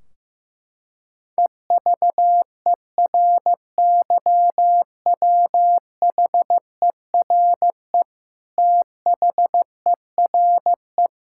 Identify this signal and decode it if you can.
Morse code, heard as a keyed tone, about 15 wpm, 700 Hz. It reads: EVERYWHERE THERE
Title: Sentences from Top 1600 Words 15wpm